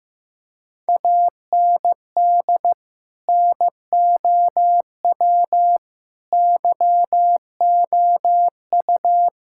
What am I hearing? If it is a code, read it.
Morse code, heard as a keyed tone, about 15 wpm, 700 Hz. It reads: AND NOW YOU
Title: Sentences from Top 1000 Words - Review Part 6 - 15wpm